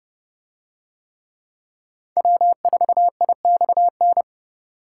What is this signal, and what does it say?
Morse code, heard as a keyed tone, about 31 wpm, 700 Hz. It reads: W4IXN